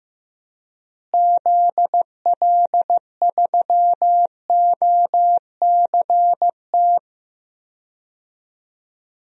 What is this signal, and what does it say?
Morse code, heard as a keyed tone, about 16 wpm, 700 Hz. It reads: ZL3OCT
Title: DXCC-100 Call Signs - 6 Characters - Repeated 3x - 15wpm